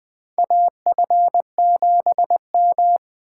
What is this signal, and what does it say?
Morse code, heard as a keyed tone, about 20 wpm, 700 Hz. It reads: AF7M
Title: 2023 SOTA Chaser Call Signs - 20wpm